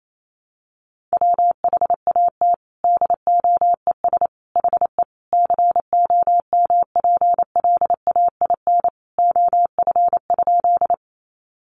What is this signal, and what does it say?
Morse code, heard as a keyed tone, about 28 wpm, 700 Hz. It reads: WHAT DOES HE COMPLAIN OF?
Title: Sentences from Top 1300 Words 28wpm